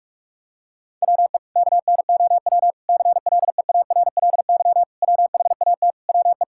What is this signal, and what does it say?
Morse code, heard as a keyed tone, about 46 wpm, 700 Hz. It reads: WE KNOW CLEARLY WHAT WE
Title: Sentences from Top 1600 Words 45wpm